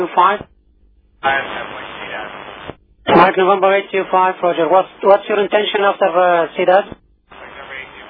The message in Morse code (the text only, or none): none